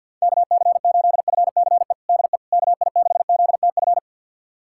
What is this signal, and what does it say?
Morse code, heard as a keyed tone, about 53 wpm, 700 Hz. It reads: KC8FCE DE KI6ZTF
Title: USA Callsign DE Callsign - 6 Characters - 50wpm